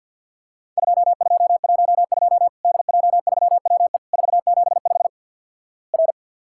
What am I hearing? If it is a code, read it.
Morse code, heard as a keyed tone, about 50 wpm, 700 Hz. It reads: 2212 DJ3WE 475 R
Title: Worked All Europe DX Contest - QTC Exchange - 50wpm